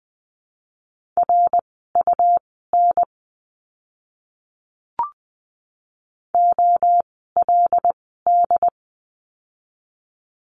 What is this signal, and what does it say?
Morse code, heard as a keyed tone, about 20 wpm, spaced as for 14 wpm, 700 Hz. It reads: RUN OLD
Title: 115 Lesson 19f - Letter w - Words 20wpm